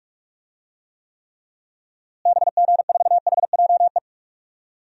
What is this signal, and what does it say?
Morse code, heard as a keyed tone, about 46 wpm, 700 Hz. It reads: BG4HJE
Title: DXCC-100 Call Signs - 6 Characters - Repeated 3x - 45wpm